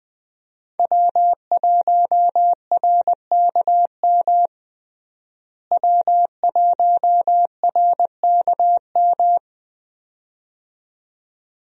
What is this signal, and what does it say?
Morse code, heard as a keyed tone, about 21 wpm, 700 Hz. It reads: W1RKM W1RKM
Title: USA Call Signs - 5 Characters - Repeated 2x - 20wpm